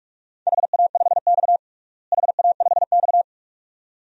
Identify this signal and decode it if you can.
Morse code, heard as a keyed tone, about 46 wpm, 700 Hz. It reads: HA5X HA5X
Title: DXCC-100 Call Signs - 4 Characters - Repeated 2x - 45wpm